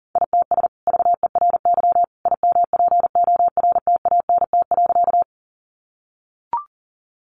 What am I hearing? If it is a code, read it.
Morse code, heard as a keyed tone, about 40 wpm, 700 Hz. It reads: ITS VERY IMPORTANT.
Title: Sentences from Top 600 Words 40wpm